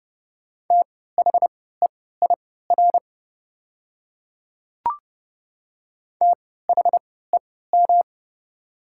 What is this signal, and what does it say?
Morse code, heard as a keyed tone, about 30 wpm, spaced as for 17 wpm, 700 Hz. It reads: THEIR THEM
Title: 3x Character Spacing - Top 100 Words - 30wpm